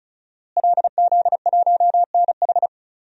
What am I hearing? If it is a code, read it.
Morse code, heard as a keyed tone, about 34 wpm, 700 Hz. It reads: LZ1NH
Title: DXCC-100 Call Signs - 5 Characters - Repeated 2x - 35wpm